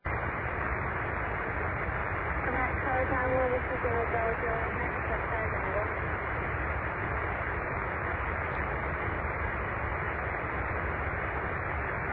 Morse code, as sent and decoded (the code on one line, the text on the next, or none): none